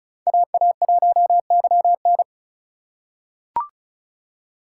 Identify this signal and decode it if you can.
Morse code, heard as a keyed tone, about 34 wpm, 700 Hz. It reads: AA1YN
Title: USA Call Signs - 5 Characters - 3x Character Spacing - 35wpm